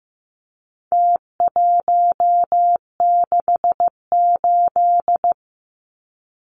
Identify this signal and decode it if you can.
Morse code, heard as a keyed tone, about 15 wpm, 700 Hz. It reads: T168
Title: Numbers - Contest Style - 4-Digit 15wpm